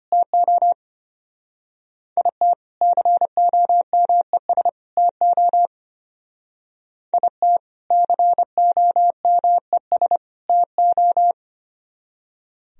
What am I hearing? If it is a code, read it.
Morse code, heard as a keyed tone, about 31 wpm, 700 Hz. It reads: TO IT COMES TO IT COMES TO
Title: Speed-Racing - Sets of 3 Words 25wpm